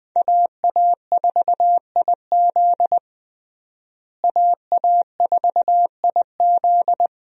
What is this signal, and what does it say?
Morse code, heard as a keyed tone, about 20 wpm, 700 Hz. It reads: AA4IZ AA4IZ
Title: USA Call Signs - 5 Characters - Repeated 3x - 20wpm